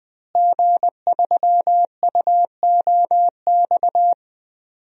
Morse code, heard as a keyed tone, about 20 wpm, 700 Hz. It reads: G3UOX
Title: DXCC-100 Call Signs - 5 Characters - Repeated 2x - 20wpm